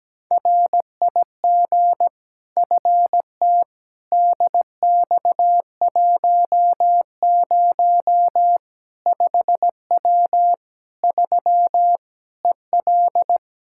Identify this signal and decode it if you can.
Morse code, heard as a keyed tone, about 17 wpm, 700 Hz. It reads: RIG FT DX10 5W 3 EL
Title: Practice QSO - Exchange 2 - Info - 17wpm